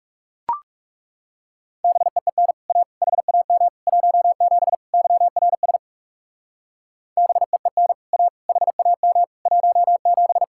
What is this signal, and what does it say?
Morse code, heard as a keyed tone, about 38 wpm, 700 Hz. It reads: BEEN A HAM 17 YRS BEEN A HAM 17
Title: Speed-Racing Common QSO Phrases - 30wpm